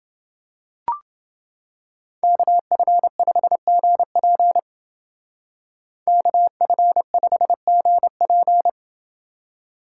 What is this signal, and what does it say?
Morse code, heard as a keyed tone, about 28 wpm, 700 Hz. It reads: KF5GP KF5GP